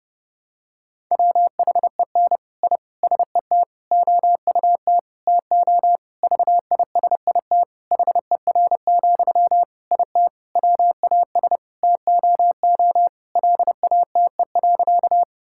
Morse code, heard as a keyed tone, about 30 wpm, 700 Hz. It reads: WHEN I SET OUT TO VISIT HER, IT WAS TOO LATE.
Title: Sentences from Top 1000 Words - Review Part 6 - 30wpm